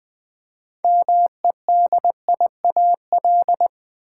Morse code, heard as a keyed tone, about 20 wpm, 700 Hz. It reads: MEDIAL
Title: Practice QSO - Exchange 3 - More Info - 20wpm